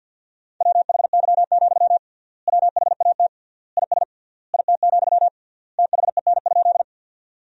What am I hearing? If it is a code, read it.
Morse code, heard as a keyed tone, about 48 wpm, 700 Hz. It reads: WHY, WHAT IS IT, THEN?
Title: 3x Word-Spacing - Sentences from Top 200 Words - 50wpm